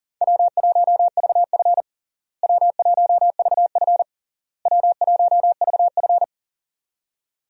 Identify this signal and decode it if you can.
Morse code, heard as a keyed tone, about 37 wpm, 700 Hz. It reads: W1VF W1VF W1VF
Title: USA Call Signs - 4 Characters - Repeated 3x - 40wpm